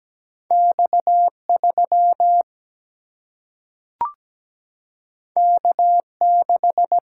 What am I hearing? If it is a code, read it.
Morse code, heard as a keyed tone, about 17 wpm, 700 Hz. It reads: X3 K6
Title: Single-Letter Single-Number 17wpm